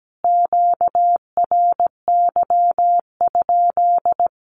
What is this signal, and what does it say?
Morse code, heard as a keyed tone, about 17 wpm, 700 Hz. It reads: QRY?
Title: Q-Codes Repeated 3x 17wpm